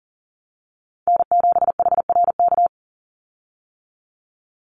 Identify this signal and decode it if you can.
Morse code, heard as a keyed tone, about 40 wpm, 700 Hz. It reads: N7HRK
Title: Call Signs - 40wpm